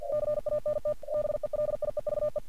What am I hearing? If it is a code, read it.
Morse code, heard as a keyed tone, about 46 wpm, 600 Hz. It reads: CANT BELIEVE
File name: archived / CRSnet